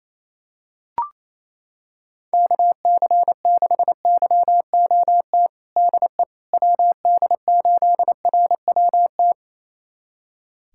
Morse code, heard as a keyed tone, about 28 wpm, 700 Hz. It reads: KC6YOT DE WD8RWT
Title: USA Callsign DE Callsign 28wpm